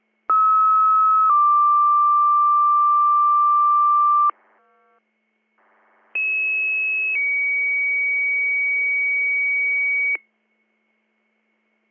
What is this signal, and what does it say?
Two-tone page, heard as a one-second tone, then a three-second tone, about 1290 then 1180 Hz, about 2600 then 2250 Hz. It is heard from 0.3 to 10.2 s.